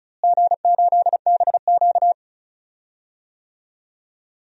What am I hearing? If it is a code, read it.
Morse code, heard as a keyed tone, about 34 wpm, 700 Hz. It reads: G8BQ
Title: DXCC-100 Call Signs - 4 Characters - Repeated 3x - 35wpm